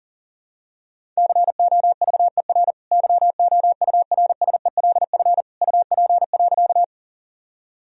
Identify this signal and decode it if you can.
Morse code, heard as a keyed tone, about 40 wpm, 700 Hz. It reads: COVER YOURSELF UP.
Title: Sentences from Top 1000 Words - Review Part 1 - 40wpm